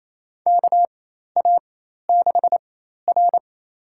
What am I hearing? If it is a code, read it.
Morse code, heard as a keyed tone, about 28 wpm, spaced as for 13 wpm, 700 Hz. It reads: KA6R
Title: USA Call Signs - 4 Characters - 4x Character Spacing - 28wpm